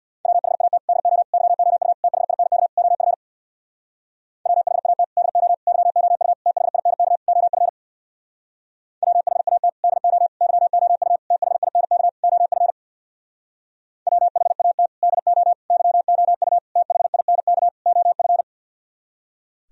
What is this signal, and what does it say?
Morse code, heard as a keyed tone, about 60 wpm, 700 Hz. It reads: WHAT DO YOU THINK OF WHAT DO YOU THINK OF WHAT DO YOU THINK OF WHAT DO YOU THINK OF
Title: Speed-Racing - Sets of 5 Words 50wpm